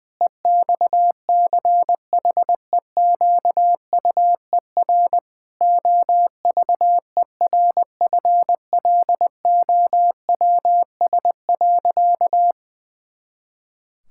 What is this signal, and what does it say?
Morse code, heard as a keyed tone, about 20 wpm, 700 Hz. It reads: EXCHEQUER OVERFLOWS.
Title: Pangram Challenge v2 - 20wpm